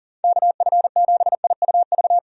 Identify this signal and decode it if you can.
Morse code, heard as a keyed tone, about 37 wpm, 700 Hz. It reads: KF7IUV
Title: USA Call Signs - 6 Characters - Repeated 2x - 40wpm